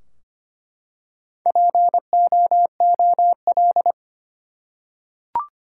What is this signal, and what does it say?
Morse code, heard as a keyed tone, about 25 wpm, 700 Hz. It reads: POOL
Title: Four Letter Words - 25wpm